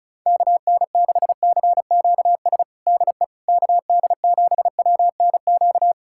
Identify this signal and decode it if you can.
Morse code, heard as a keyed tone, about 36 wpm, 690 Hz. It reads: KN6CQS DE KD7WNQ
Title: USA Callsign DE Callsign - 6 Characters - 35wpm